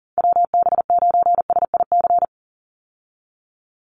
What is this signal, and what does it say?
Morse code, heard as a keyed tone, about 41 wpm, 700 Hz. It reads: WB9SIC
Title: USA Call Signs 40wpm